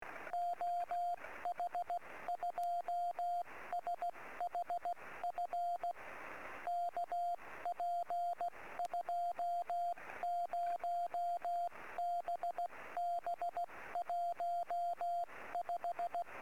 Morse code, heard as a keyed tone, about 16 wpm, 700 Hz, with radio noise. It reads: OH2SHF KP20BB15
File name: sm7lcb / oh2shf 23-13-3cm